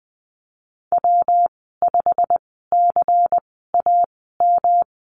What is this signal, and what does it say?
Morse code, heard as a keyed tone, about 20 wpm, spaced as for 15 wpm, 700 Hz. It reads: W5CAM
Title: 140 Lesson 23g - Letter f - Call Signs 20wpm